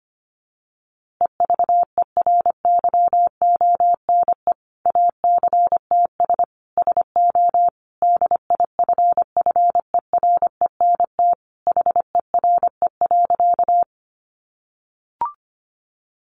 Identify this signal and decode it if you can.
Morse code, heard as a keyed tone, about 25 wpm, 700 Hz. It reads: EVERYONE ACTS SO DIFFERENT HERE.
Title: Easy Phrases 25wpm